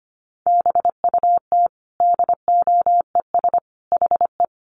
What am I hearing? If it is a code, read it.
Morse code, heard as a keyed tone, about 25 wpm, 700 Hz. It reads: BUT DOES HE
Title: Sentences from Top 1200 Words 25wpm